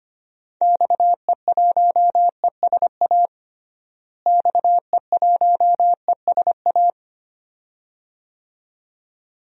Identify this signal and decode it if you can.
Morse code, heard as a keyed tone, about 25 wpm, 700 Hz. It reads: XE1ESA XE1ESA
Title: DXCC-100 Call Signs - 6 Characters - Repeated 2x - 25wpm